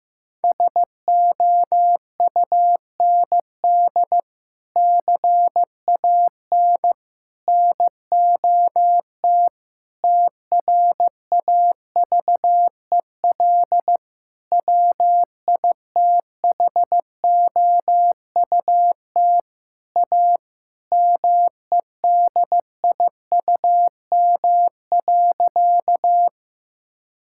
Morse code, heard as a keyed tone, about 15 wpm, 700 Hz. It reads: SOUND CAN NOT TRAVEL WITHOUT A MEDIUM.
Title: Sentences from Top 1000 Words - Review Part 6 - 15wpm